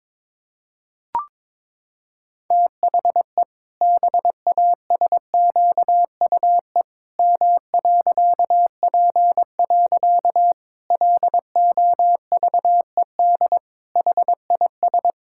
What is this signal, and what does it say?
Morse code, heard as a keyed tone, about 22 wpm, 700 Hz. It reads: THE BASQUE M.P. LOVED HIS
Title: Pangram Challenge v2 - 22wpm